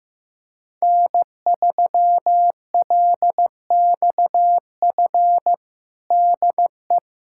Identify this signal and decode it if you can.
Morse code, heard as a keyed tone, about 14 wpm, 700 Hz. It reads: N3LXF DE
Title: USA Callsign DE Callsign - 5 Characters - 15wpm